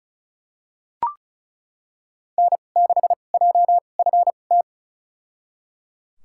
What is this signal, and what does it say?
Morse code, heard as a keyed tone, about 36 wpm, spaced as for 25 wpm, 700 Hz. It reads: N6JFT
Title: USA Call Signs - 5 Characters - 2x Character Spacing - 35wpm